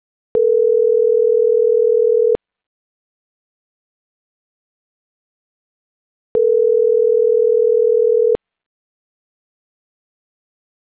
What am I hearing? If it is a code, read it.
Ringback tone (North America), 440 + 480 Hz.